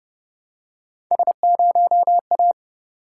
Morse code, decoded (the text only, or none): S0A